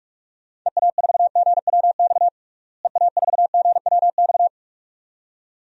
Morse code, heard as a keyed tone, about 46 wpm, 700 Hz. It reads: EA4GWX EA4GWX
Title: DXCC-100 Call Signs - 6 Characters - Repeated 2x - 45wpm